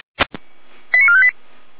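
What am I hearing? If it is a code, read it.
Five-tone selective call (ZVEI), heard as quick runs of five tones, tones of about 70 ms. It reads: 89458
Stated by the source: ZVEI five-tone call received on SDR, logged by Feuerwehr Orlamünde.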